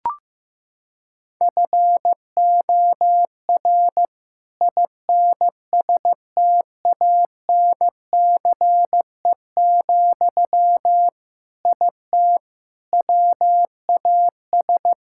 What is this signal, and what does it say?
Morse code, heard as a keyed tone, about 15 wpm, 700 Hz. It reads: FOR INSTANCE, IT WAS
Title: Sentences from Top 1000 Words - Review Part 5 - 15wpm